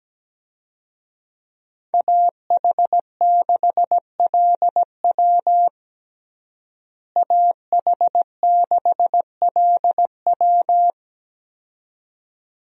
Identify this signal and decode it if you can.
Morse code, heard as a keyed tone, about 17 wpm, 700 Hz. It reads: AH6LW AH6LW
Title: USA Call Signs - 5 Characters - Repeated 2x - 17wpm